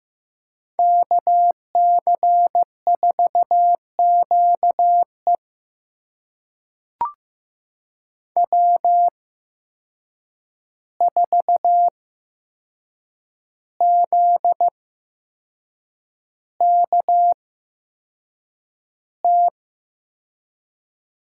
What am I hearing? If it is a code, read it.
Morse code, heard as a keyed tone, about 15 wpm, spaced as for 6 wpm, 700 Hz. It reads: KC4QE W4ZKT
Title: USA Call Signs - 8x Character Spacing - 15wpm